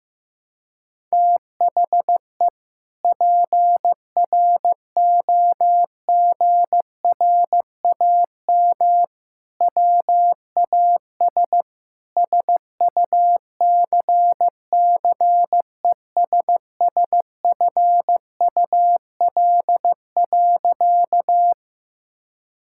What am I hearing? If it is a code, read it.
Morse code, heard as a keyed tone, about 15 wpm, 700 Hz. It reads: THE PROGRAM WAS SUCCESSFUL.
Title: Sentences from Top 1000 Words - Review Part 5 - 15wpm